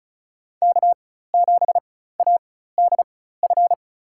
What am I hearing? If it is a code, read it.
Morse code, heard as a keyed tone, about 34 wpm, spaced as for 16 wpm, 700 Hz. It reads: K7ADF
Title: USA Call Signs - 5 Characters - 4x Character Spacing - 35wpm